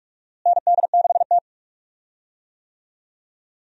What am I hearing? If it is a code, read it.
Morse code, heard as a keyed tone, about 46 wpm, 700 Hz. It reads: ND6T
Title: USA Call Signs - 4 Characters - Repeated 2x - 45wpm